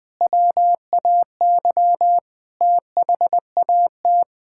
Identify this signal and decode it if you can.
Morse code, heard as a keyed tone, about 20 wpm, 700 Hz. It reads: WAY THAT
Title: Sentences with 7-Word Phrases - Spoken Beginning - 20wpm